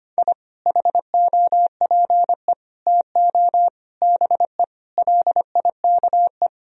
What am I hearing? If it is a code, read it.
Morse code, heard as a keyed tone, about 25 wpm, 700 Hz. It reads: I HOPE TO BE LIKE
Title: Sentences from Top 1000 Words - Review Part 1 - 25wpm